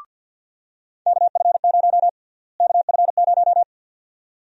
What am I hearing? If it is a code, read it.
Morse code, heard as a keyed tone, about 52 wpm, 700 Hz. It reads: KF0 KF0
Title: USA Call Sign Prefixes - Repeated 2x - 50wpm